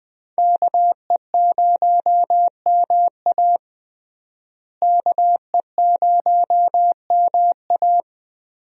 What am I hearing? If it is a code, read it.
Morse code, heard as a keyed tone, about 20 wpm, 700 Hz. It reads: KE0MA KE0MA
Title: USA Call Signs - 5 Characters - Repeated 3x - 20wpm